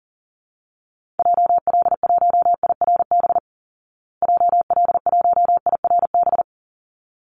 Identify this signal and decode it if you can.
Morse code, heard as a keyed tone, about 38 wpm, 700 Hz. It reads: JL1IRB JL1IRB
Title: Call Signs Repeated 3x - 40wpm